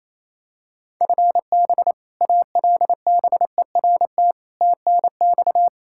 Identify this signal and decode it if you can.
Morse code, heard as a keyed tone, about 28 wpm, 700 Hz. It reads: FB ALBERT TNX
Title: Practice QSO - Exchange 3 - More Info - 28wpm